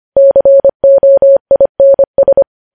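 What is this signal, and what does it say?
Morse code, heard as a keyed tone, about 25 wpm, 550 Hz. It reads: COINS
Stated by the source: Morse code news headlines sent at 25 words per minute.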